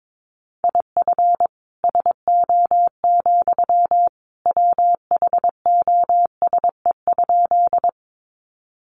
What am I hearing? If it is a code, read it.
Morse code, heard as a keyed tone, about 22 wpm, 700 Hz. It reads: IF SO, WHOSE?
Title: Sentences from Top 1200 Words 22wpm